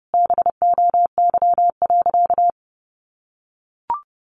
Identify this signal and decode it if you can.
Morse code, heard as a keyed tone, about 30 wpm, 700 Hz. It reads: BOY.
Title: Sentences from Top 500 Words 30wpm